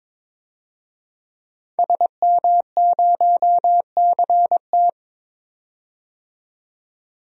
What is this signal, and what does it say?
Morse code, heard as a keyed tone, about 21 wpm, 700 Hz. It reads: SM0CT
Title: DXCC-100 Call Signs - 5 Characters - 22wpm